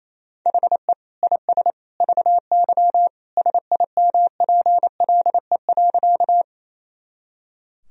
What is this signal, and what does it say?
Morse code, heard as a keyed tone, about 28 wpm, 700 Hz. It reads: HE IS VY SIMPLE.
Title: Sentences from Top 400 Words and Abbreviations 28wpm